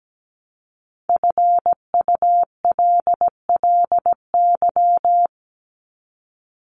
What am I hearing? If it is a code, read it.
Morse code, heard as a keyed tone, about 17 wpm, 700 Hz. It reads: FULLY
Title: Top 1001-1250 Words 17wpm